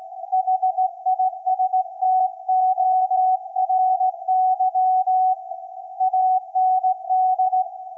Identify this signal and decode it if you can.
Morse code, heard as a keyed tone, about 16 wpm, 750 Hz. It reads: HISTORY AND
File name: iz7slz / VLF / SAQ 24-12-2024